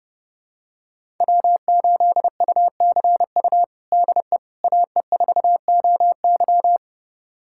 Morse code, heard as a keyed tone, about 32 wpm, 700 Hz. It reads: W8UCU DE AE4OY
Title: USA Callsign DE Callsign 30wpm